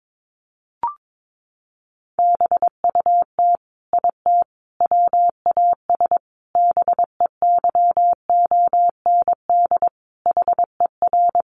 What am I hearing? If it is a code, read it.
Morse code, heard as a keyed tone, about 22 wpm, 700 Hz. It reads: BUT IT WAS BEYOND HER
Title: Sentences from Top 1200 Words 22wpm